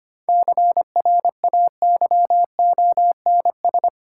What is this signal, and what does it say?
Morse code, heard as a keyed tone, about 25 wpm, 700 Hz. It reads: CRAYONS
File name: practice / 25wpm / Easy Words with Hints - 25wpm